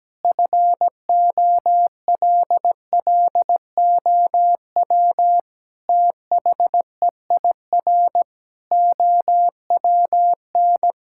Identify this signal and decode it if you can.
Morse code, heard as a keyed tone, about 17 wpm, 700 Hz. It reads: FOLLOW THEIR OWN
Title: Sentences from Top 1000 Words - Review Part 6 - 17wpm